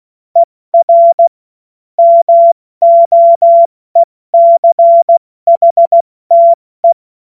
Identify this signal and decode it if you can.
Morse code, heard as a keyed tone, about 16 wpm, 680 Hz. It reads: ER MOECHTE
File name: Morsekurs / Witze / Witz 06. September 2022 70 F80